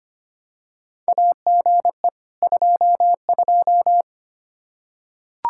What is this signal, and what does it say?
Morse code, heard as a keyed tone, about 25 wpm, 700 Hz. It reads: AGE 22
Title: Practice QSO - Exchange 3 - More Info - 25wpm